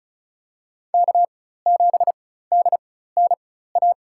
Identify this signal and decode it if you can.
Morse code, heard as a keyed tone, about 34 wpm, spaced as for 16 wpm, 700 Hz. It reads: K7DNA
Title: USA Call Signs - 5 Characters - 4x Character Spacing - 35wpm